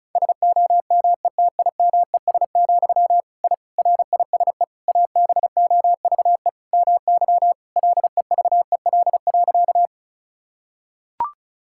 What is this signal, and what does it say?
Morse code, heard as a keyed tone, about 34 wpm, 700 Hz. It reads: SOMETIMES, I RISE ABOVE MY LEVEL.
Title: Sentences from Top 1000 Words - Review Part 6 - 35wpm